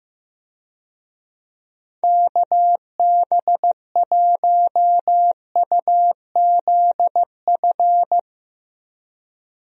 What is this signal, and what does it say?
Morse code, heard as a keyed tone, about 15 wpm, 700 Hz. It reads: KB1UZF
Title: USA Call Signs - 6 Characters - Repeated 2x - 15wpm